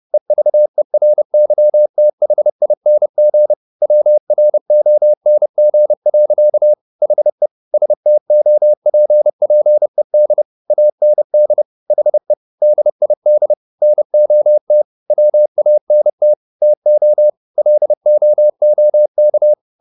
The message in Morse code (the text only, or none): EVERYTHING WRONG. HE STOPPED AND HE DID NOT WANT TO LOOK